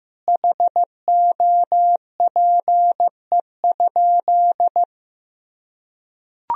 Morse code, heard as a keyed tone, about 16 wpm, 700 Hz. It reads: HOPE?
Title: Sentences from Top 1000 Words - Review Part 3 - 15wpm